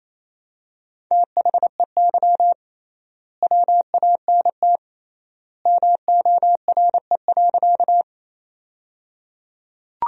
Morse code, heard as a keyed tone, about 28 wpm, 700 Hz. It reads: THEY WANT MORE.
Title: 3x Word-Spacing - Sentences from Top 200 Words - 28wpm